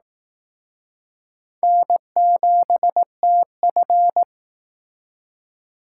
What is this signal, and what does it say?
Morse code, heard as a keyed tone, about 18 wpm, 700 Hz. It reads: N7TF
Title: Speed Racing - USA 1x2 Call Signs 15wpm